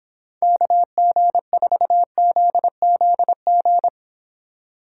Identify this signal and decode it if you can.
Morse code, heard as a keyed tone, about 25 wpm, 700 Hz. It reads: KG4ZZG